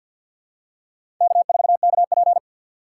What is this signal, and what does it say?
Morse code, heard as a keyed tone, about 53 wpm, 700 Hz. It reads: K4KP
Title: USA Call Signs - 4x Character Spacing - 50wpm